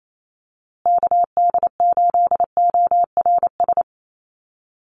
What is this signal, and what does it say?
Morse code, heard as a keyed tone, about 28 wpm, 700 Hz. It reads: KD8ORS
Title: USA Call Signs Repeated 2x 28wpm